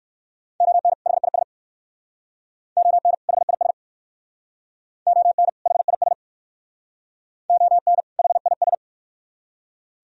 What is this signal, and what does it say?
Morse code, heard as a keyed tone, about 59 wpm, 700 Hz. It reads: ON HIS ON HIS ON HIS ON HIS